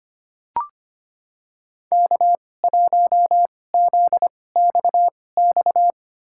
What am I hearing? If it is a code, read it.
Morse code, heard as a keyed tone, about 25 wpm, spaced as for 18 wpm, 700 Hz. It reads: K1ZXX
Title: USA Call Signs - 5 Characters - 2x Character Spacing - 25wpm